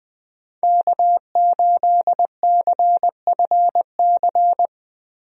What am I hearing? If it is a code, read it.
Morse code, heard as a keyed tone, about 21 wpm, 700 Hz. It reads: K8CFC
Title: USA Call Signs - 5 Characters - 20wpm